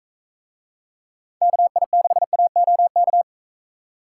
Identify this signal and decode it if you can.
Morse code, heard as a keyed tone, about 44 wpm, 700 Hz. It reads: KI6AOK